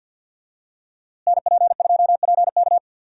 Morse code, heard as a keyed tone, about 52 wpm, 700 Hz. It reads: NP2PK
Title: USA Call Signs - 5 Characters - Repeated 2x - 50wpm